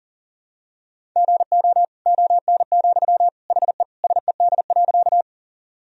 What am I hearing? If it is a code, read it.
Morse code, heard as a keyed tone, about 37 wpm, 700 Hz. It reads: GO ON, HE SED.
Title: Sentences from Top 100 Words and Abbreviations 40wpm